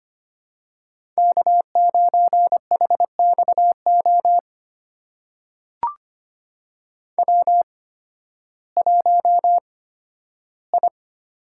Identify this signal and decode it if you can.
Morse code, heard as a keyed tone, about 25 wpm, spaced as for 13 wpm, 700 Hz. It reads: K9HXO W1I
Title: USA Call Signs - 5 Characters - 8x Character Spacing - 25wpm